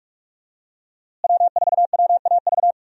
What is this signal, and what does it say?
Morse code, heard as a keyed tone, about 41 wpm, 700 Hz. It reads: W4WAV
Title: USA Call Signs - 5 Characters - Repeated 2x - 45wpm